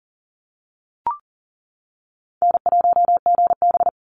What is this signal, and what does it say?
Morse code, heard as a keyed tone, about 40 wpm, 700 Hz. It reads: N1GB
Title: NA SOTA Chaser Callsigns 40wpm